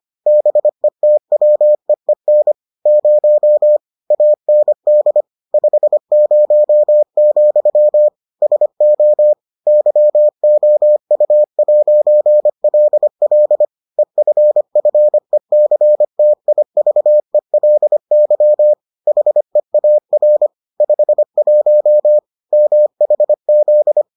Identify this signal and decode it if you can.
Morse code, heard as a keyed tone, about 25 wpm, 600 Hz. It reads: BETWEEN 0 AND 50, SO YOU'LL EFFECTIVELY HEAR 51 MHZ